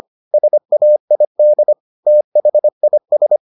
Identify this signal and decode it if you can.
Morse code, heard as a keyed tone, about 25 wpm, 600 Hz. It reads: SAID THIS